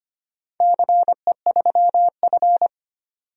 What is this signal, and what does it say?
Morse code, heard as a keyed tone, about 25 wpm, 700 Hz. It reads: CE3F
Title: DXCC-100 Call Signs - 4 Characters - 25wpm